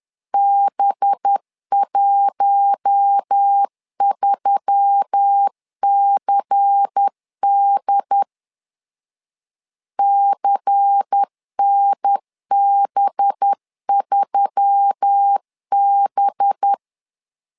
Morse code, heard as a keyed tone, about 10 wpm, 800 Hz. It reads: B13CD CNB3B
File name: LEZIONI / 5 LBVC / 5 LBVC 10wpm 2349